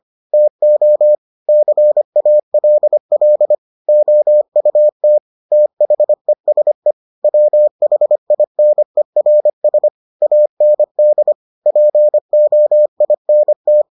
Morse code, heard as a keyed tone, about 25 wpm, 600 Hz. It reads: TO CALL OUT THESE WHINERS AND POINT